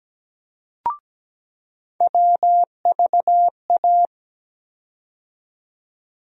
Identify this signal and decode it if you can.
Morse code, heard as a keyed tone, about 17 wpm, 700 Hz. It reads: WVA
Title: USA Call Sign Postfixes - 3 Letters 17wpm